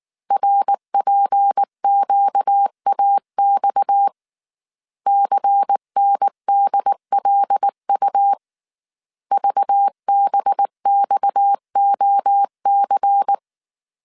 Morse code, heard as a keyed tone, about 19 wpm, 800 Hz. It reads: RPQAX CNDLU VBXOC
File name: CW / DET / 1 6 1500C S A 18wpm 2255